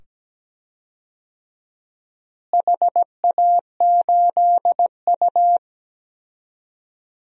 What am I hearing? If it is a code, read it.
Morse code, heard as a keyed tone, about 17 wpm, 700 Hz. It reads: HA8U
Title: DXCC-100 Call Signs - 4 Characters - Repeated 2x - 17wpm